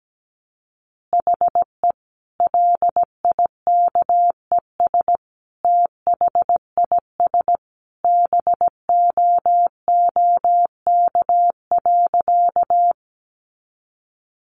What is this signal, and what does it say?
Morse code, heard as a keyed tone, about 17 wpm, 700 Hz. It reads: HE LIKES THIS BOOK.
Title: Additional Easy Phrases 17wpm